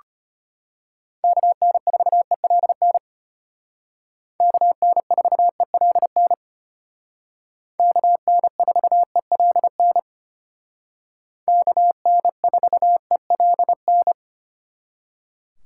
Morse code, heard as a keyed tone, about 30 wpm, 700 Hz. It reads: KN4ELN KN4ELN KN4ELN KN4ELN
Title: Speed Racing - USA 2x3 Call Signs 25wpm